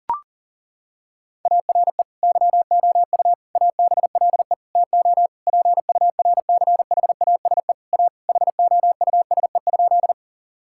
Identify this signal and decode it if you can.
Morse code, heard as a keyed tone, about 40 wpm, 700 Hz. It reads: ARE YOU ABLE TO PURCHASE A HOUSE?
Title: Sentences from Top 1000 Words - Review Part 5 - 40wpm